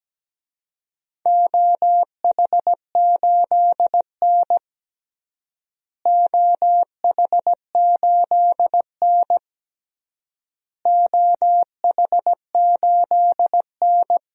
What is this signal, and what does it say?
Morse code, heard as a keyed tone, about 17 wpm, 700 Hz. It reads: OH8N OH8N OH8N
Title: DXCC-100 Call Signs - 4 Characters - Repeated 3x - 17wpm